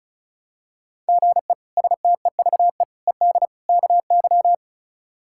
Morse code, heard as a keyed tone, about 36 wpm, 700 Hz. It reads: GE STEVE ED KY
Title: K1USN SST QSO Practice - 35wpm